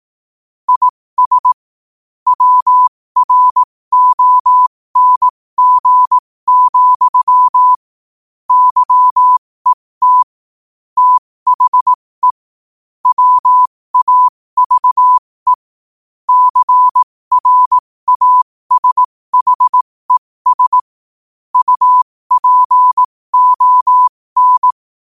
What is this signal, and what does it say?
Morse code, heard as a keyed tone, about 18 wpm, spaced as for 15 wpm, 1000 Hz. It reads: IS WRONG, YET THE WAVE CRASHES UPON
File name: qotdfiles / 15wpm